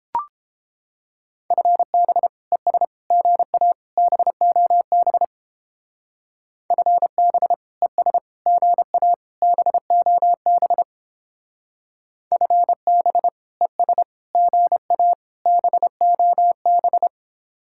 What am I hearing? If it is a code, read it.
Morse code, heard as a keyed tone, about 29 wpm, 700 Hz. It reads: FB ES GA BOB FB ES GA BOB FB ES GA BOB